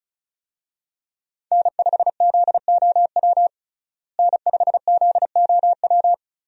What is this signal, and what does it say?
Morse code, heard as a keyed tone, about 34 wpm, 700 Hz. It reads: N5ZOW N5ZOW
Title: USA Call Signs - 5 Characters - Repeated 2x - 35wpm